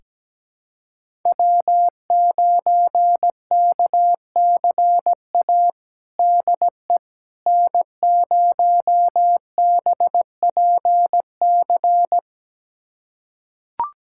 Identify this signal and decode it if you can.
Morse code, heard as a keyed tone, about 17 wpm, 700 Hz. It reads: W9KCA DE N0BPC
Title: USA Callsign DE Callsign - 5 Characters - 17wpm